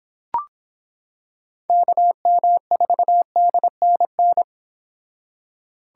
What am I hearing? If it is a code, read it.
Morse code, heard as a keyed tone, about 25 wpm, 700 Hz. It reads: KM4DNN